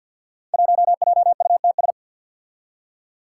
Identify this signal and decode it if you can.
Morse code, heard as a keyed tone, about 48 wpm, 700 Hz. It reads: 1JUTS
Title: Five Challenging Letter-Number Combinations 50wpm